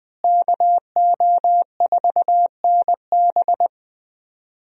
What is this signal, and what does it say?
Morse code, heard as a keyed tone, about 19 wpm, 700 Hz. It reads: KO4NB
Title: USA Call Signs - 5 Characters - Repeated 3x - 20wpm